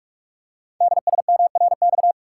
Morse code, heard as a keyed tone, about 44 wpm, 700 Hz. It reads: DSMRX
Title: Code Groups - 5 Letters - 45wpm